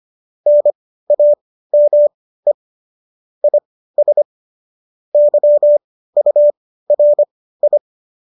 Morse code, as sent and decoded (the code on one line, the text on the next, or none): -. .- -- . / .. ... / -.-- ..- .-. ..
NAME IS YURI